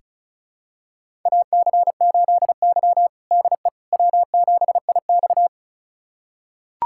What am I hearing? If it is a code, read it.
Morse code, heard as a keyed tone, about 36 wpm, 700 Hz. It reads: AC8Y DE W7IX
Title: USA Callsign DE Callsign - 4 Characters - 35wpm